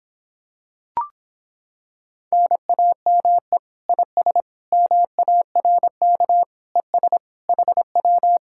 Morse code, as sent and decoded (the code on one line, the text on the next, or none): -. .- -- . / .. ... / -- .- .-. -.- / . ... / .... .--
NAME IS MARK ES HW